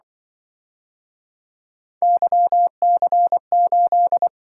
Morse code, heard as a keyed tone, about 24 wpm, 700 Hz. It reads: YC8
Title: Speed-Racing - Three Letter-Number Combinations - 20wpm